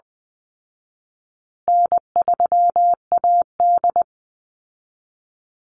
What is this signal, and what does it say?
Morse code, heard as a keyed tone, about 20 wpm, 700 Hz. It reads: N3AD
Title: Easy Call Signs Repeated 3x - 20wpm